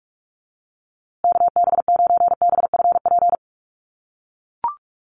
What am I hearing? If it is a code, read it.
Morse code, heard as a keyed tone, about 44 wpm, 700 Hz. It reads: KB9BFP
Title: USA Call Signs Repeated 3x 45wpm